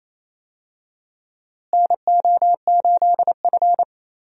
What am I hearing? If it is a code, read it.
Morse code, heard as a keyed tone, about 28 wpm, 700 Hz. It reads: NO8F
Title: USA Call Signs - 4 Characters - Repeated 3x - 28wpm